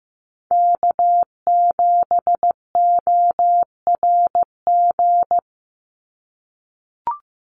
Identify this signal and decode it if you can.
Morse code, heard as a keyed tone, about 15 wpm, 700 Hz. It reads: K7ORG
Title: NA SOTA Chaser Callsigns 15wpm